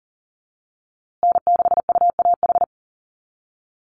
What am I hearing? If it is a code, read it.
Morse code, heard as a keyed tone, about 38 wpm, 700 Hz. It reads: N6UAH
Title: USA Call Signs Repeated 2x 40wpm